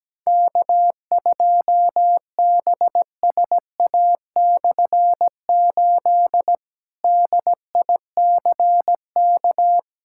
Morse code, heard as a keyed tone, about 17 wpm, 700 Hz. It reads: K2BSA/8 DICK
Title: K1USN SST QSO Practice - 17wpm